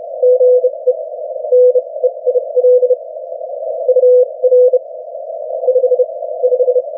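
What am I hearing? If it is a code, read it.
Morse code, heard as a keyed tone, about 32 wpm, spaced as for 24 wpm, 500 Hz. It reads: GE NEIL UR 5 5
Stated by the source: Morse from the New Zealand Net, the national amateur radio net of Aotearoa/New Zealand, meeting daily on 3535.0 kHz.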